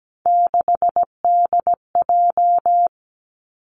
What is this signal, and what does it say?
Morse code, heard as a keyed tone, about 17 wpm, 700 Hz. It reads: 6DJ
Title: Three Challenging Letter-Number Combinations 17wpm